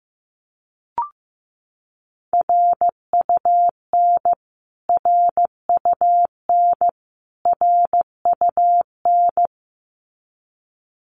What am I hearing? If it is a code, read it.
Morse code, heard as a keyed tone, about 15 wpm, 700 Hz. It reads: RUN RUN RUN
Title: Top 500 Words Repeated 15wpm